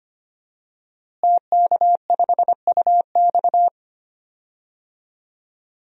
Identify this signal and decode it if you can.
Morse code, heard as a keyed tone, about 25 wpm, 700 Hz. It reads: TK5UX
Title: DXCC-100 Call Signs - 5 Characters - Repeated 3x - 25wpm